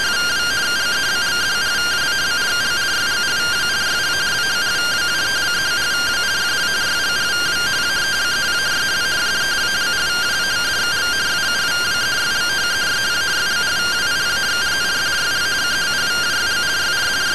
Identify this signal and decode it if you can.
Two-tone teleprinter signal, about 170 Hz shift.